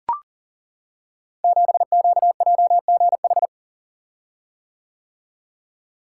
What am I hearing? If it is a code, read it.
Morse code, heard as a keyed tone, about 39 wpm, 690 Hz. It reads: 7QJGH
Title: Code Groups - 5 Letter:Numbers - 40wpm